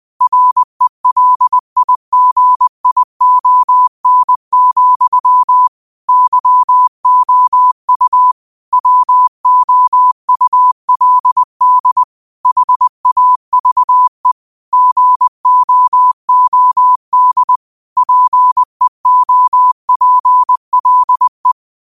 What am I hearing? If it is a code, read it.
Morse code, heard as a keyed tone, about 20 wpm, 1000 Hz. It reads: RELIGION, YOU WOULD HAVE GOOD PEOPLE
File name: qotdfiles / 20wpm